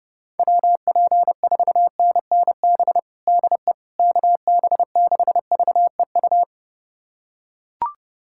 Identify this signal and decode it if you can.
Morse code, heard as a keyed tone, about 29 wpm, 700 Hz. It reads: WP4NNB DE KB6VEU
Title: USA Callsign DE Callsign - 6 Characters - 30wpm